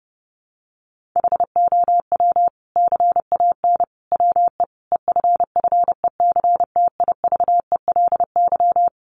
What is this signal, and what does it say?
Morse code, heard as a keyed tone, about 30 wpm, 700 Hz. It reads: HOW CAN WE EFFECTIVELY
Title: Sentences from Top 1400 Words 30wpm